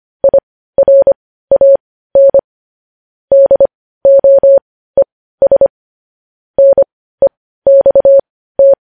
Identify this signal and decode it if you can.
Morse code, heard as a keyed tone, about 25 wpm, spaced as for 15 wpm, 550 Hz. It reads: IRAN DOES NEXT